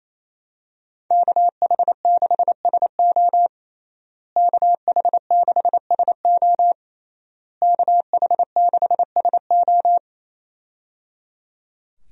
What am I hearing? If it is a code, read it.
Morse code, heard as a keyed tone, about 27 wpm, 700 Hz. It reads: KH6SO KH6SO KH6SO
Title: USA Call Signs - 5 Characters - Repeated 3x - 28wpm